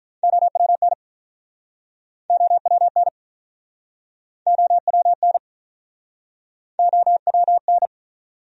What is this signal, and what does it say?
Morse code, heard as a keyed tone, about 49 wpm, 700 Hz. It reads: OWN OWN OWN OWN